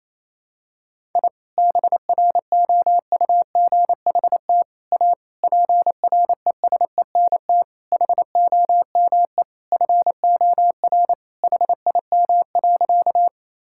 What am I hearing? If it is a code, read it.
Morse code, heard as a keyed tone, about 28 wpm, 700 Hz. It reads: I BROUGHT A PRESENT HOME FOR HIM.
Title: Sentences from Top 1000 Words - Review Part 5 - 28wpm